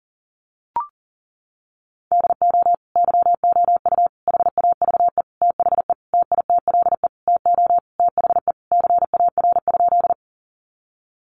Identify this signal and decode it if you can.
Morse code, heard as a keyed tone, about 37 wpm, 700 Hz. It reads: DO YOU HAVE THE TITLE TO THE CAR?
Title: Sentences from Top 1200 Words 40wpm